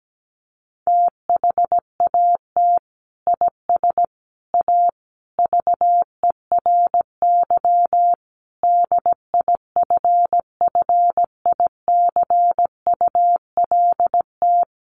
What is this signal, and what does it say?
Morse code, heard as a keyed tone, about 17 wpm, 700 Hz. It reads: THAT IS A VERY DIFFICULT
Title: Sentences from Top 1200 Words 17wpm